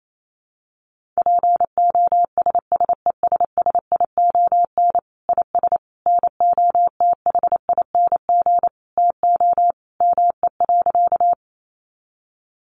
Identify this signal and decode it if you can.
Morse code, heard as a keyed tone, about 28 wpm, 700 Hz. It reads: POSSESSION IS NOTHING TO ME.
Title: Sentences from Top 1400 Words 28wpm